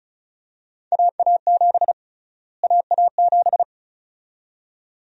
Morse code, heard as a keyed tone, about 36 wpm, 700 Hz. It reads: AA7 AA7
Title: USA Call Sign Prefixes - Repeated 2x - 35wpm